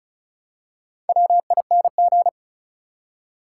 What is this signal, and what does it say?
Morse code, heard as a keyed tone, about 36 wpm, 700 Hz. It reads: WING
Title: Four Letter Words - 35wpm